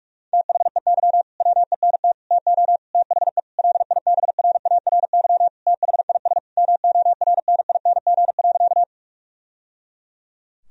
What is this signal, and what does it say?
Morse code, heard as a keyed tone, about 42 wpm, 700 Hz. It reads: THEY WENT TO THE LIBRARY THIS MORNING.
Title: Sentences from Top 1000 Words - Review Part 4 - 45wpm